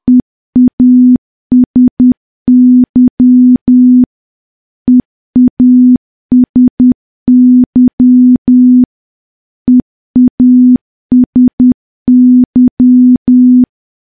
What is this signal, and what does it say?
Morse code, heard as a keyed tone, about 10 wpm, 250 Hz. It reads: EASY EASY EASY
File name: morse / morse-training-10WPM